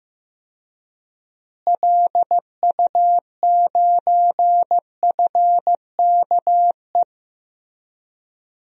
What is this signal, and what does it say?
Morse code, heard as a keyed tone, about 15 wpm, 700 Hz. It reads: LU9FKE
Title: DXCC-100 Call Signs - 6 Characters - Repeated 3x - 15wpm